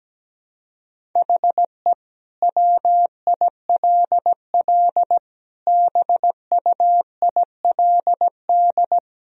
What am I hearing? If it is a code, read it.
Morse code, heard as a keyed tone, about 17 wpm, 700 Hz. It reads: HE WILL BUILD
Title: Sentences from Top 1000 Words - Review Part 1 - 17wpm